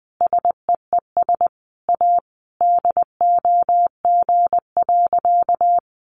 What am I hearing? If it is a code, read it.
Morse code, heard as a keyed tone, about 20 wpm, 700 Hz. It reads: SEES A DOG.
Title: Easy Phrases 20wpm